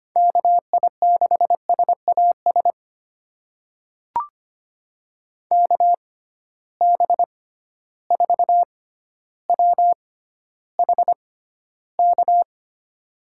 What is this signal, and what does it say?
Morse code, heard as a keyed tone, about 25 wpm, spaced as for 13 wpm, 700 Hz. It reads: KI6SAS KB4WHK
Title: USA Call Signs - 6 Characters - 6x Character Spacing - 25wpm